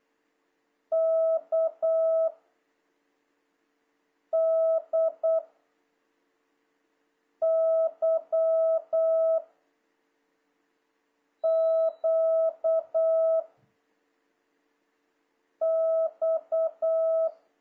Morse code, heard as a keyed tone, about 8 wpm, 650 Hz. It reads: K D Y Q X
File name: jh8cbh / 2026cw